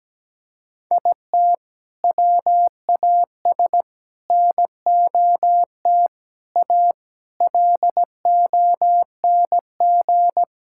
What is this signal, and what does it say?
Morse code, heard as a keyed tone, about 17 wpm, 700 Hz. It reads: IT WAS NOT A LONG
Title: Sentences from Top 1000 Words - Review Part 6 - 17wpm